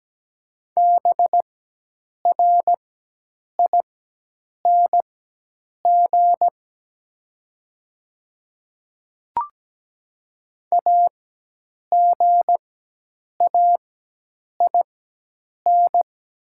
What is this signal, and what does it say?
Morse code, heard as a keyed tone, about 17 wpm, spaced as for 8 wpm, 700 Hz. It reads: BRING AGAIN
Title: 4x Character Spacing - Top 500 Words - 17wpm